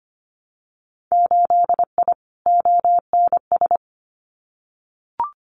Morse code, heard as a keyed tone, about 25 wpm, 690 Hz. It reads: 8I ONS
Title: Winter Field Day Exchanges Encore 25wpm